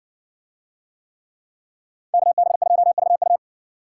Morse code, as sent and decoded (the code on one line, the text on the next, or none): -.- -... ..--- ...- ..-
KB2VU